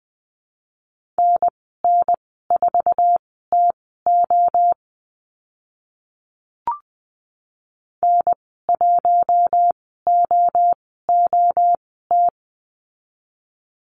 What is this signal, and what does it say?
Morse code, heard as a keyed tone, about 20 wpm, spaced as for 14 wpm, 700 Hz. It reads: NN4TO N1OOT
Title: 082 Lesson 14g - Number 2 - Call Signs 20wpm